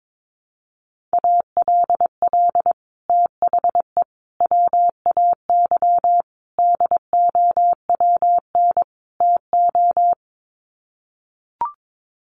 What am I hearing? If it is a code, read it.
Morse code, heard as a keyed tone, about 22 wpm, 700 Hz. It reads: ALL THE WAY DOWN TO
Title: Sets of 5 Words Encore2 22wpm